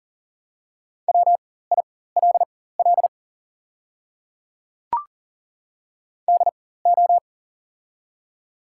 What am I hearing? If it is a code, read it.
Morse code, heard as a keyed tone, about 40 wpm, spaced as for 19 wpm, 690 Hz. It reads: WILL DO
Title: 4x Character Spacing - Top 100 Words - 40wpm